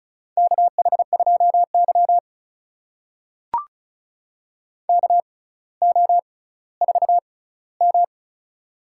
Morse code, heard as a keyed tone, about 36 wpm, spaced as for 18 wpm, 700 Hz. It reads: KH2Y KO4M
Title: USA Call Signs - 4 Characters - 6x Character Spacing - 35wpm